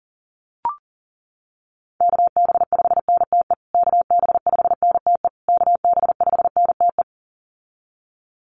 Morse code, heard as a keyed tone, about 37 wpm, 700 Hz. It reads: KB5NTE KB5NTE KB5NTE
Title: USA Call Signs Repeated 3x 40wpm